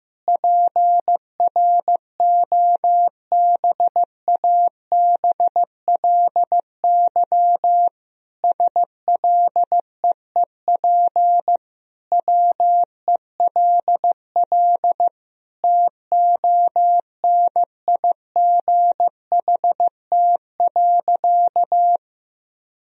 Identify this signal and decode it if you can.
Morse code, heard as a keyed tone, about 15 wpm, 700 Hz. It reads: PROBABLY SLEEP WELL TONIGHT.
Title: Sentences from Top 1000 Words - Review Part 4 - 15wpm